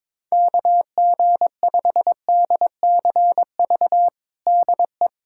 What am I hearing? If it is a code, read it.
Morse code, heard as a keyed tone, about 22 wpm, 700 Hz. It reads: KG5DCV DE
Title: USA Callsign DE Callsign - 6 Characters - 22wpm